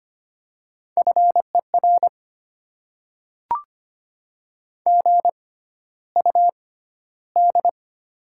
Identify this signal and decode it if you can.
Morse code, heard as a keyed tone, about 25 wpm, spaced as for 13 wpm, 700 Hz. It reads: FER GUD
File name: Top 20 QSO Elements - 6x Character Spacing - 25wpm